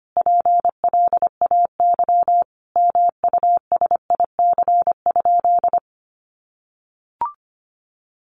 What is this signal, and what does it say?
Morse code, heard as a keyed tone, about 25 wpm, 700 Hz. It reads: PLAY MUSIC?
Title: Sentences from Top 1000 Words 25wpm